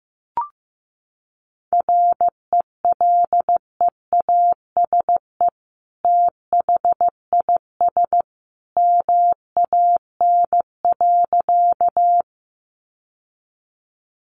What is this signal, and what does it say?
Morse code, heard as a keyed tone, about 15 wpm, 700 Hz. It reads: RELEASE THIS MAN.
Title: Sentences from Top 1100 Words 15wpm